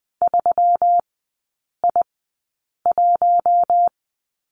Morse code, heard as a keyed tone, about 20 wpm, spaced as for 8 wpm, 690 Hz. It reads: 3I1
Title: 145 Lesson 24e - Letter y - Three Characters 20wpm